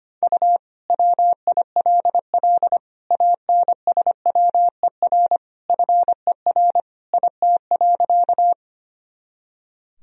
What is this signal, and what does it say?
Morse code, heard as a keyed tone, about 25 wpm, 700 Hz. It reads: U WILL ANSWER FER IT.
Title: Sentences from Top 300 Words and Abbreviations 25wpm